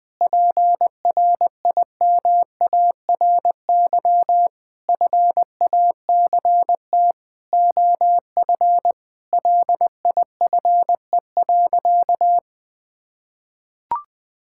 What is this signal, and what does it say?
Morse code, heard as a keyed tone, about 20 wpm, 700 Hz. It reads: PRIMARY FACT OF LIFE.
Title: Sentences from Top 1000 Words - Review Part 5 - 20wpm